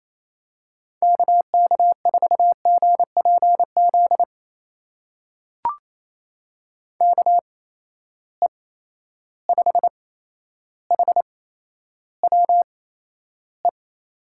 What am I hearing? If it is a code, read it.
Morse code, heard as a keyed tone, about 28 wpm, spaced as for 12 wpm, 700 Hz. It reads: KK4GPZ KE5HWE
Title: USA Call Signs - 6 Characters - 8x Character Spacing - 28wpm